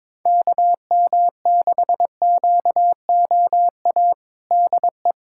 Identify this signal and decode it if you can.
Morse code, heard as a keyed tone, about 22 wpm, 700 Hz. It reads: KM6QOA DE
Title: USA Callsign DE Callsign - 6 Characters - 22wpm